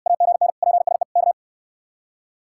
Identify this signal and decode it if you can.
Morse code, heard as a keyed tone, about 63 wpm, 700 Hz. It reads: AGN PSE K